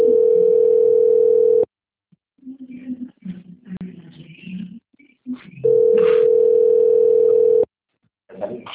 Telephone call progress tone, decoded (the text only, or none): ringback tone